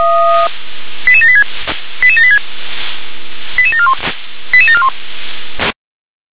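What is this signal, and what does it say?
Five-tone selective call (ZVEI), heard as quick runs of five tones, tones of about 70 ms. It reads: 88786 88786 88741 88741
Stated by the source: ZVEI five-tone selective call received on SDR.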